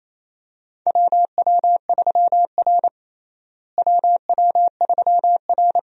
Morse code, heard as a keyed tone, about 28 wpm, 700 Hz. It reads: WW3R WW3R
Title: USA Call Signs - 4 Characters - Repeated 2x - 28wpm